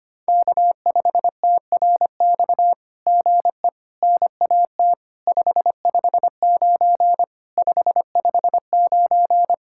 Morse code, heard as a keyed tone, about 24 wpm, 700 Hz. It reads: K5TRX GE NAT 559 559
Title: SOTA QSO Practice 25wpm